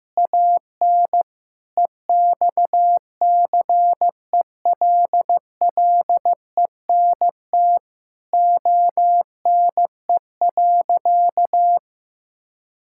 Morse code, heard as a keyed tone, about 15 wpm, 700 Hz. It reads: AN EXCELLENT ONE.
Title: Sentences from Top 1000 Words - Review Part 6 - 15wpm